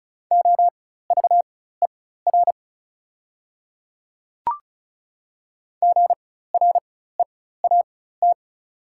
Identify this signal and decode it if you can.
Morse code, heard as a keyed tone, about 36 wpm, spaced as for 16 wpm, 700 Hz. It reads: OVER GREAT
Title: 4x Character Spacing - Top 200 Words - 35wpm